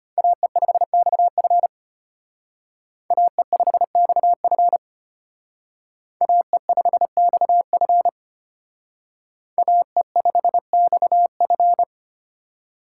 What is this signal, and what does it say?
Morse code, heard as a keyed tone, about 33 wpm, 700 Hz. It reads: AE5XF AE5XF AE5XF AE5XF